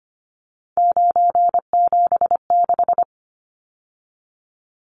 Morse code, heard as a keyed tone, about 25 wpm, 700 Hz. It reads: 976
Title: Numbers - 3-Digit 25wpm